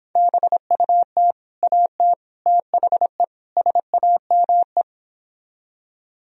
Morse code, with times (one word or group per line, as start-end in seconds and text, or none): 0.2-1.3 BUT
1.6-2.1 AT
2.5-3.2 THE
3.6-4.8 SAME